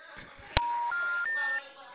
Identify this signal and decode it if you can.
Special information tone, with noise.
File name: out- -2004- - - .785